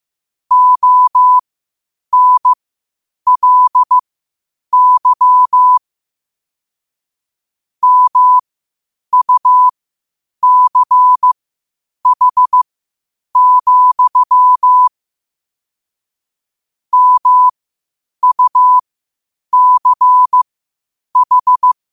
Morse code, heard as a keyed tone, about 15 wpm, spaced as for 8 wpm, 1000 Hz. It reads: ONLY MUCH, MUCH
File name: qotdfiles / 8wpm